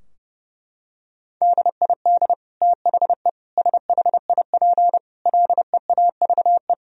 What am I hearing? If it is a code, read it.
Morse code, heard as a keyed tone, about 30 wpm, 700 Hz. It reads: DID THE SHIP LEAVE
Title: Sentences from Top 1000 Words - Review Part 2 - 30wpm